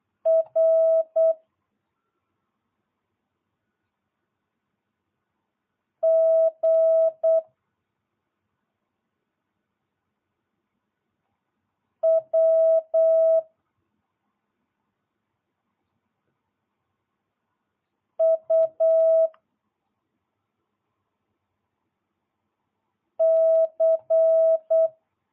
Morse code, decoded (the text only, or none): R G W U C